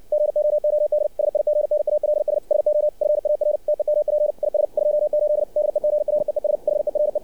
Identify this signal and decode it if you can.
Morse code, heard as a keyed tone, about 59 wpm, 600 Hz. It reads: GOOD SIGNALS SO FAR TEMP IS 28 DEGREES HERE
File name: archived / CRSnet